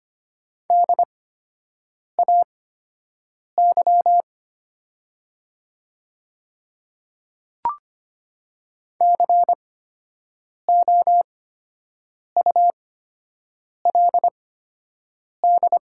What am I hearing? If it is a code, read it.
Morse code, heard as a keyed tone, about 25 wpm, spaced as for 7 wpm, 700 Hz. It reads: DAY COULD
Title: 8x Character Spacing - Top 100 Words - 25wpm